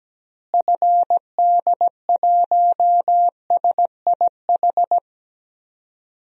Morse code, heard as a keyed tone, about 17 wpm, 700 Hz. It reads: FD1SIH
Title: DXCC-100 Call Signs - 6 Characters - Repeated 3x - 17wpm